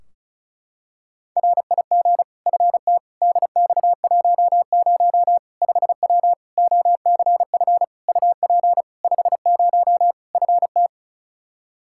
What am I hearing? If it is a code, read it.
Morse code, heard as a keyed tone, about 36 wpm, 700 Hz. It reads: RIG FT DX10 5W OCF UP 50 FT
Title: Practice QSO - Exchange 2 - Info - 35wpm